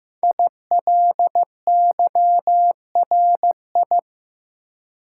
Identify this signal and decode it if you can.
Morse code, heard as a keyed tone, about 15 wpm, 700 Hz. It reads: ILYRI
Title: Code Groups - 5 Letters - 15wpm